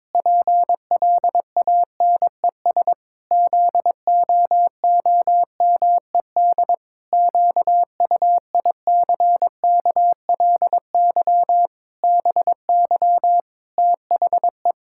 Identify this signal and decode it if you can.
Morse code, heard as a keyed tone, about 22 wpm, 700 Hz. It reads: PLANES ZOOMED QUICKLY BY THE
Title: Pangram Challenge v2 - 22wpm